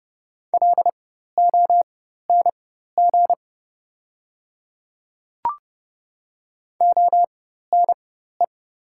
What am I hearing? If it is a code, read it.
Morse code, heard as a keyed tone, about 29 wpm, spaced as for 14 wpm, 690 Hz. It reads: LONG ONE
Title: 4x Character Spacing - Top 100 Words - 30wpm